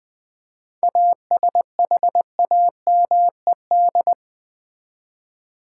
Morse code, heard as a keyed tone, about 21 wpm, 700 Hz. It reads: ASHAMED
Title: 7 Letter Words - First 3 Letters Given - 20wpm